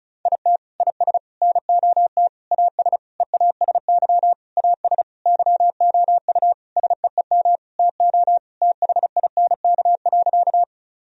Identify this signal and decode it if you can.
Morse code, heard as a keyed tone, about 36 wpm, 700 Hz. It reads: IT IS NOT AS EASY AS YOU SEEM TO THINK.
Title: Sentences from Top 1000 Words - Review Part 1 - 35wpm